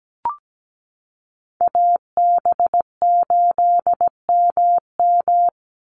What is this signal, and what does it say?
Morse code, heard as a keyed tone, about 17 wpm, 700 Hz. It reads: AB8MM
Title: USA Call Signs 17wpm